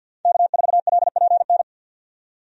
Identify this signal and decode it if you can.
Morse code, heard as a keyed tone, about 45 wpm, 700 Hz. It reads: K4LPN
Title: USA Call Signs - 5 Characters - 50wpm